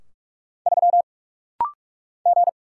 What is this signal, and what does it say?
Morse code, heard as a keyed tone, about 48 wpm, 700 Hz. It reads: 3 G
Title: Single Letter-Number - Rapid-Fire 45wpm